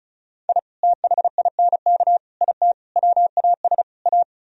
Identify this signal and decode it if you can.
Morse code, heard as a keyed tone, about 42 wpm, 700 Hz. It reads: I THINK IT WAS A
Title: Speed-Racing - Sets of 5 Words 35wpm